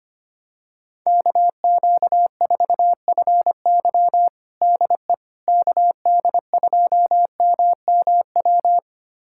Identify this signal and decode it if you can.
Morse code, heard as a keyed tone, about 25 wpm, 700 Hz. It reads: KQ4FY DE KD2MMW
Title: USA Callsign DE Callsign 25wpm